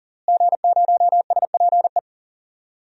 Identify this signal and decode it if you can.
Morse code, heard as a keyed tone, about 40 wpm, 700 Hz. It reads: G0SPE
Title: DXCC-100 Call Signs - 5 Characters - Repeated 3x - 40wpm